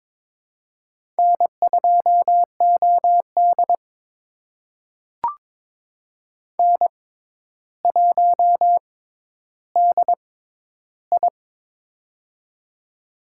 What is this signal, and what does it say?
Morse code, heard as a keyed tone, about 22 wpm, spaced as for 11 wpm, 700 Hz. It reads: N2OD N1DI
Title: USA Call Signs - 4 Characters - 6x Character Spacing - 22wpm